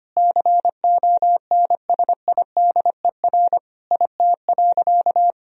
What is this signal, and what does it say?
Morse code, heard as a keyed tone, about 25 wpm, 700 Hz. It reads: CONSIDER IT.
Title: Sentences from Top 1000 Words - Review Part 4 - 25wpm